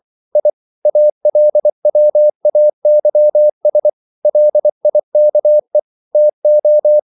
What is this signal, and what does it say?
Morse code, heard as a keyed tone, about 24 wpm, 600 Hz. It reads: I ALWAYS LIKE TO
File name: CW / Practice / EnglishQuotations0324